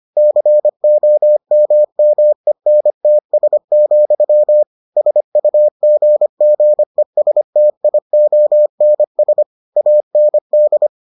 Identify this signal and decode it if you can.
Morse code, heard as a keyed tone, about 25 wpm, 600 Hz. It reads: COMMENTS, SUGGESTIONS AND